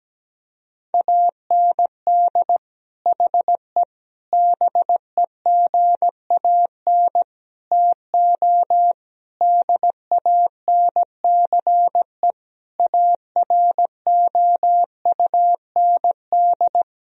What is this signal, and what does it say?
Morse code, heard as a keyed tone, about 17 wpm, 700 Hz. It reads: AND HE BEGAN TO DANCE AROUND
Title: Sentences from Top 1000 Words - Review Part 4 - 17wpm